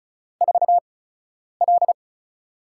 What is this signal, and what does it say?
Morse code, heard as a keyed tone, about 36 wpm, spaced as for 10 wpm, 700 Hz. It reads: 4L
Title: USA Call Signs - 4 Characters - 8x Character Spacing - 35wpm